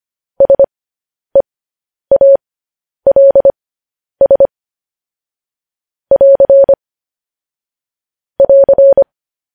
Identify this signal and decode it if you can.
Morse code, heard as a keyed tone, about 25 wpm, spaced as for 10 wpm, 550 Hz. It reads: SEALS <AR> <AR>
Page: news headlines in morse code at 10 wpm